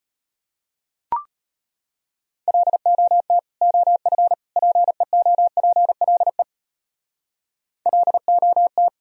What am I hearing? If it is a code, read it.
Morse code, heard as a keyed tone, about 33 wpm, 700 Hz. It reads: LOT OF PEOPLE LOT